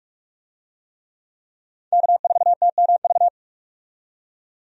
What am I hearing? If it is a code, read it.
Morse code, heard as a keyed tone, about 46 wpm, 700 Hz. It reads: K4TMV
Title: USA Call Signs - 5 Characters - 45wpm